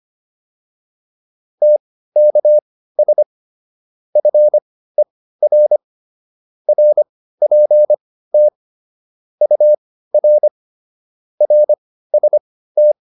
Morse code, heard as a keyed tone, about 25 wpm, spaced as for 15 wpm, 600 Hz. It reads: TKS FER RPT UR RST